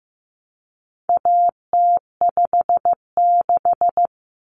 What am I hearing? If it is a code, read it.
Morse code, heard as a keyed tone, about 15 wpm, 700 Hz. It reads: AT56
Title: Common Cut Numbers - 4-Digit 15wpm